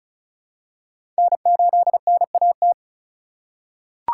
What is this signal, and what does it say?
Morse code, heard as a keyed tone, about 34 wpm, 700 Hz. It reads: N8NAT
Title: USA Call Signs - 4x Character Spacing - 35wpm